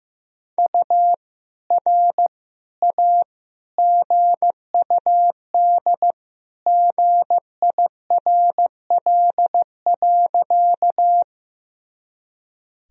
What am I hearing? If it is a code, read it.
Morse code, heard as a keyed tone, about 15 wpm, 700 Hz. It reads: U R A GUD GIRL.
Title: Sentences from Top 400 Words and Abbreviations 15wpm